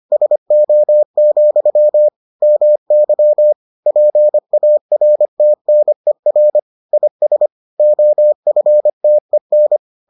Morse code, heard as a keyed tone, about 25 wpm, 600 Hz. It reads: SO, MY PARTNER IS OFTEN